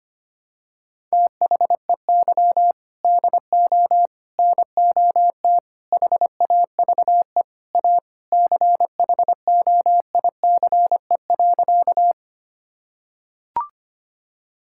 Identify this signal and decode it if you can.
Morse code, heard as a keyed tone, about 25 wpm, 700 Hz. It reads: THEY DO NOT HAVE A CHOICE.
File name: practice / 25wpm / Sentences with 4-Word Phrases - 25wpm